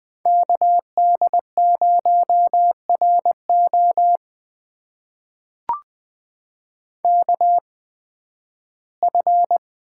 Morse code, heard as a keyed tone, about 20 wpm, spaced as for 13 wpm, 700 Hz. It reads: KD0RO KF
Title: USA Call Signs - 5 Characters - 8x Character Spacing - 20wpm